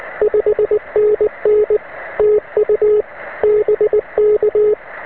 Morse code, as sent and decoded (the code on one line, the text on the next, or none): ..... -. -. / - ..- / -... -.-
5NN TU BK